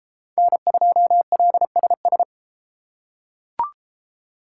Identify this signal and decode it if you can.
Morse code, heard as a keyed tone, about 33 wpm, 700 Hz. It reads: N2LSS